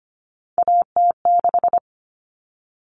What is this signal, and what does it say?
Morse code, heard as a keyed tone, about 25 wpm, 700 Hz. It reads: AT6
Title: All Cut Numbers - 3-Digit 25wpm